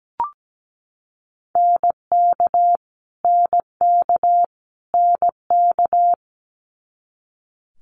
Morse code, heard as a keyed tone, about 17 wpm, 700 Hz. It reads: NK NK NK
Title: Multi-letter Phonemes Repeated 3x 17wpm